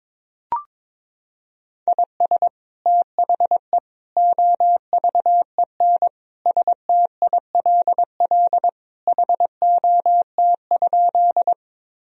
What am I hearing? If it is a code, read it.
Morse code, heard as a keyed tone, about 22 wpm, 700 Hz. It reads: IS THE OVEN STILL HOT?
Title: Sentences from Top 1000 Words - Review Part 4 - 22wpm